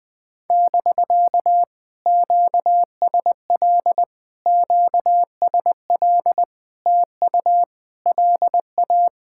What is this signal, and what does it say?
Morse code, heard as a keyed tone, about 19 wpm, 700 Hz. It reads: <BK> QSL QSL TU LA
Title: POTA QSO Practice - 20wpm